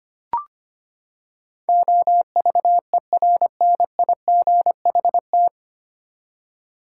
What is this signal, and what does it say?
Morse code, heard as a keyed tone, about 25 wpm, 700 Hz. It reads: OVERNIGHT
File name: Nine Letter Words - 25wpm